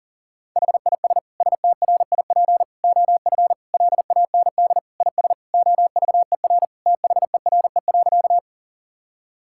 Morse code, heard as a keyed tone, about 40 wpm, 700 Hz. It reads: HIS STRIP OF LAND IS OVER THERE.
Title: Sentences from Top 1600 Words 40wpm